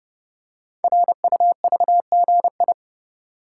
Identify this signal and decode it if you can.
Morse code, heard as a keyed tone, about 29 wpm, 700 Hz. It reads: RUVGI
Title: Code Groups - 5 Letter:Numbers - 30wpm